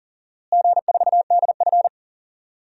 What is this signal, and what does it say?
Morse code, heard as a keyed tone, about 40 wpm, 700 Hz. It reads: G4DF
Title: DXCC-100 Call Signs - 4 Characters - Repeated 2x - 40wpm